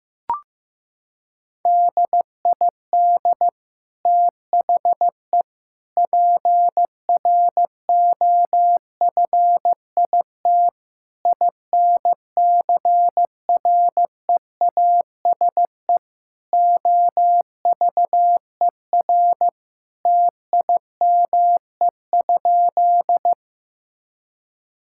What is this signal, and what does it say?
Morse code, heard as a keyed tone, about 15 wpm, 700 Hz. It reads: DID THE PROFIT INCREASE OVER TIME?
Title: Sentences from Top 1000 Words - Review Part 5 - 15wpm